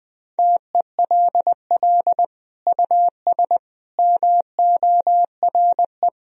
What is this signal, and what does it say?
Morse code, heard as a keyed tone, about 20 wpm, 700 Hz. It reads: TELL US MORE
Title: Sentences from Top 1000 Words - Review Part 4 - 20wpm